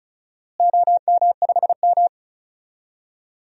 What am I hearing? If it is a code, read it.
Morse code, heard as a keyed tone, about 36 wpm, 700 Hz. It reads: OM5M
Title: DXCC-100 Call Signs - 4 Characters - Repeated 2x - 35wpm